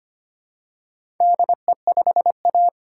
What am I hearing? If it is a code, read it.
Morse code, heard as a keyed tone, about 25 wpm, 700 Hz. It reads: DE5A
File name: practice / 25wpm / Four Letter-Number Combinations - 25wpm